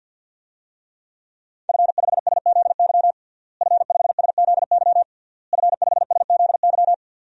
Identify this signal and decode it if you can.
Morse code, heard as a keyed tone, about 48 wpm, 700 Hz. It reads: F5SZY F5SZY F5SZY
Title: DXCC-100 Call Signs - 5 Characters - Repeated 3x - 50wpm